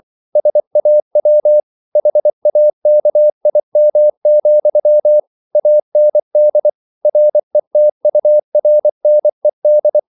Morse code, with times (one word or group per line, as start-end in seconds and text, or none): 0.4-1.6 SAW
2.0-5.2 HAKIM,
5.5-6.7 AND
7.0-10.0 RETURNED